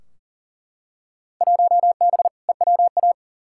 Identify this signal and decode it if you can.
Morse code, heard as a keyed tone, about 37 wpm, 700 Hz. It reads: 1B EWA
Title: Field Day Exchanges Encore - 40wpm